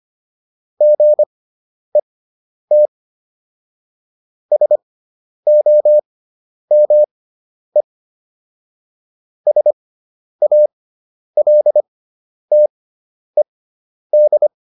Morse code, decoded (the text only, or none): GET SOME SALTED